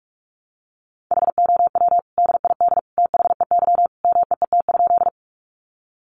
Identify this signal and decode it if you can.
Morse code, heard as a keyed tone, about 44 wpm, 700 Hz. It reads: HOW DID THEY MEET?
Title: Sentences from Top 1100 Words 45wpm